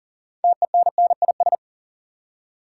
Morse code, heard as a keyed tone, about 40 wpm, 700 Hz. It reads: TENNIS
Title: Six Letter Words - 40wpm